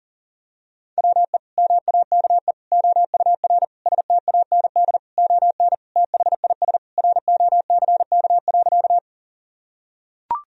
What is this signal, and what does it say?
Morse code, heard as a keyed tone, about 40 wpm, 700 Hz. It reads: WE MAKE OUR STAND ON THIS ROCK.
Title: Sentences from Top 1000 Words - Review Part 2 - 40wpm